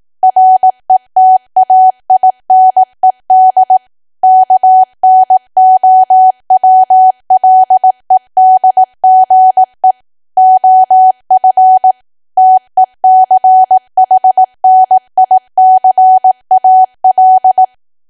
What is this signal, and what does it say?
Morse code, heard as a keyed tone, about 18 wpm, 750 Hz. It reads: RETAINED KNOWLEDGE OF TECHNICAL
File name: cwa / W1AW-Level2 / 18WPM